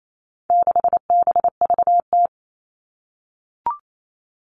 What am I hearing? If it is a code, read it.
Morse code, heard as a keyed tone, about 28 wpm, 700 Hz. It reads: 6BVT
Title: All Cut Numbers - 4-Digit 28wpm